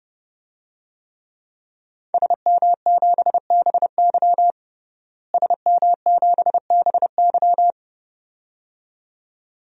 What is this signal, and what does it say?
Morse code, heard as a keyed tone, about 30 wpm, 700 Hz. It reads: SM7BY SM7BY
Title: DXCC-100 Call Signs - 5 Characters - Repeated 2x - 30wpm